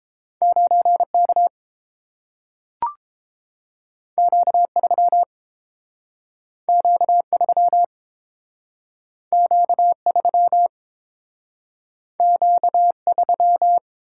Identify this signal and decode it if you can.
Morse code, heard as a keyed tone, about 30 wpm, 700 Hz. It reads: 9K Q3 Q3 Q3 Q3